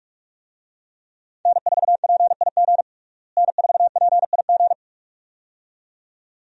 Morse code, heard as a keyed tone, about 46 wpm, 700 Hz. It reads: N4PIG N4PIG
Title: USA Call Signs - 5 Characters - Repeated 2x - 45wpm